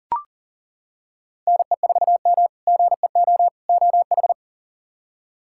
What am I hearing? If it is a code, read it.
Morse code, heard as a keyed tone, about 37 wpm, 700 Hz. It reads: NE4M GEO OH
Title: K1USN SST QSO Practice - 40wpm